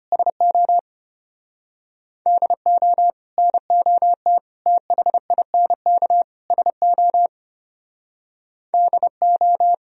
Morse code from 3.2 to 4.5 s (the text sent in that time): NOT